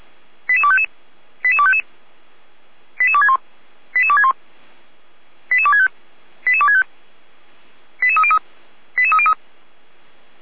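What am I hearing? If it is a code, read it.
Five-tone selective call (ZVEI), heard as quick runs of five tones, tones of about 70 ms. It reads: 88255 88255 88261 88261 88265 88265 88303 88303
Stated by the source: Five-tone ZVEI selective call received by SDR.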